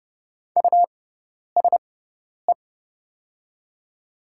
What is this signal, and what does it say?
Morse code, heard as a keyed tone, about 30 wpm, spaced as for 10 wpm, 700 Hz. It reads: USE
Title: 6x Character Spacing - Top 100 Words - 30wpm